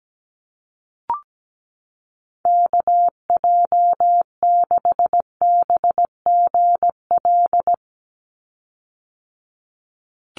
Morse code, heard as a keyed tone, about 17 wpm, 700 Hz. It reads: KJ6BGL
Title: USA Call Signs 17wpm